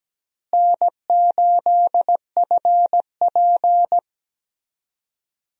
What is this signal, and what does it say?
Morse code, heard as a keyed tone, about 17 wpm, 700 Hz. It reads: N8FP
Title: USA Call Signs - 4 Characters - Repeated 3x - 17wpm